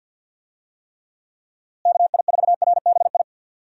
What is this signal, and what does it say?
Morse code, heard as a keyed tone, about 50 wpm, 700 Hz. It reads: KI4RBI
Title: USA Call Signs - 6 Characters - Repeated 2x - 50wpm